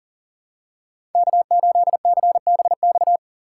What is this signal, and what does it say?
Morse code, heard as a keyed tone, about 40 wpm, 700 Hz. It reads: K8CBX
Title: USA Call Signs - 5 Characters - 3x Character Spacing - 40wpm